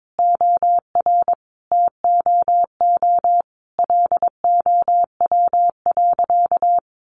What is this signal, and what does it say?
Morse code, heard as a keyed tone, about 22 wpm, 700 Hz. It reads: OR TOO LOW.
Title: Sentences from Top 1200 Words 22wpm